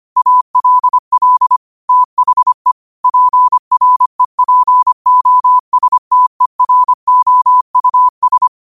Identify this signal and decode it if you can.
Morse code, heard as a keyed tone, about 25 wpm, 1000 Hz. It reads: ALL THE PREPOSTEROUS